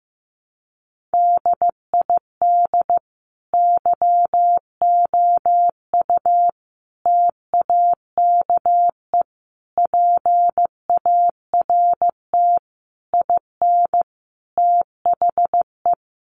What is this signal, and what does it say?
Morse code, heard as a keyed tone, about 15 wpm, 700 Hz. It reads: DID YOU TAKE PART IN THE
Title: Sentences from Top 1200 Words 15wpm